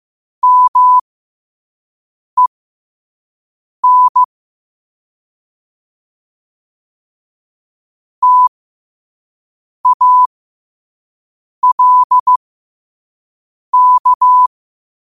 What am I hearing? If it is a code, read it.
Morse code, heard as a keyed tone, about 15 wpm, spaced as for 5 wpm, 1000 Hz. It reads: MEN TALK
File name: qotdfiles / 5wpm